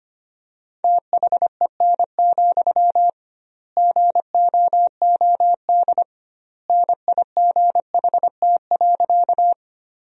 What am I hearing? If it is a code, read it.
Morse code, heard as a keyed tone, about 25 wpm, 700 Hz. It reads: THEN, GOOD NIGHT.
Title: 2x Word-Spacing - Sentences from Top 300 Words - 25wpm